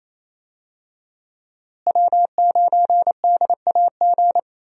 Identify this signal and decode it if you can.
Morse code, heard as a keyed tone, about 28 wpm, 700 Hz. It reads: W9DAG
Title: USA Call Signs - 5 Characters - Repeated 3x - 28wpm